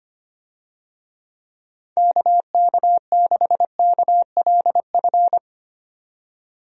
Morse code, heard as a keyed tone, about 25 wpm, 700 Hz. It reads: KK6KLF
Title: USA Call Signs - 6 Characters - 25wpm